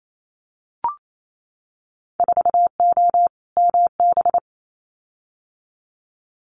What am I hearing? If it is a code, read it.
Morse code, heard as a keyed tone, about 28 wpm, 700 Hz. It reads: 4O MB
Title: Winter Field Day Exchanges Encore 28wpm